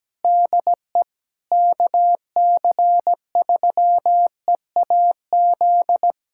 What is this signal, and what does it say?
Morse code, heard as a keyed tone, about 17 wpm, 700 Hz. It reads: DE KC3EAZ
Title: Practice QSO - Exchange 4 - Ending - 17wpm